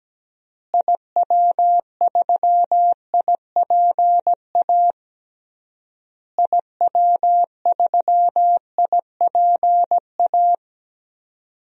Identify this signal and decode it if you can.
Morse code, heard as a keyed tone, about 17 wpm, 700 Hz. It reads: IW3IPA IW3IPA
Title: DXCC-100 Call Signs - 6 Characters - Repeated 3x - 17wpm